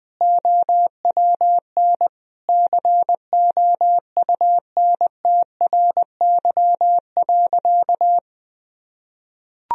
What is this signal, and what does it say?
Morse code, heard as a keyed tone, about 20 wpm, 700 Hz. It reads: OWN COUNTRY.
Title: Sentences from Top 1000 Words - Review Part 1 - 20wpm